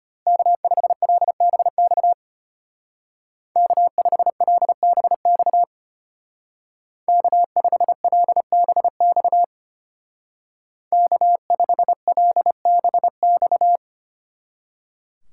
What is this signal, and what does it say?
Morse code, heard as a keyed tone, about 33 wpm, 700 Hz. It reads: K5LBX K5LBX K5LBX K5LBX